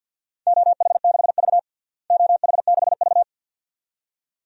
Morse code, heard as a keyed tone, about 52 wpm, 700 Hz. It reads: OH6V OH6V
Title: DXCC-100 Call Signs - 4 Characters - Repeated 2x - 50wpm